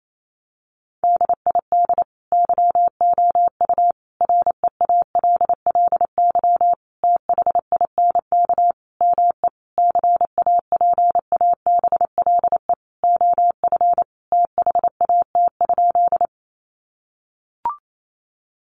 Morse code, heard as a keyed tone, about 28 wpm, 700 Hz. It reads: DID YOU REALLY THINK ME CAPABLE OF THAT?
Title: Sentences from Top 1200 Words 28wpm